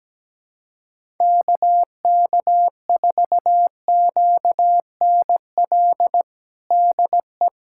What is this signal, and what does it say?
Morse code, heard as a keyed tone, about 17 wpm, 700 Hz. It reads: KK4QNL DE
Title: USA Callsign DE Callsign - 6 Characters - 17wpm